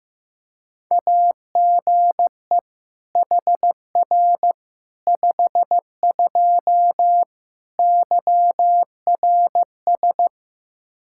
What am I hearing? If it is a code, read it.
Morse code, heard as a keyed tone, about 16 wpm, 700 Hz. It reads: AGE HR 52 YRS
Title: Practice QSO - Exchange 3 - More Info - 15wpm